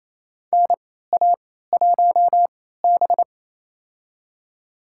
Morse code, heard as a keyed tone, about 28 wpm, spaced as for 16 wpm, 700 Hz. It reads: NA1B
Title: USA Call Signs - 4 Characters - 3x Character Spacing - 28wpm